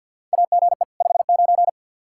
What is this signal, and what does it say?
Morse code, heard as a keyed tone, about 52 wpm, 700 Hz. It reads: AGE 59
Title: Practice QSO - Exchange 3 - More Info - 50wpm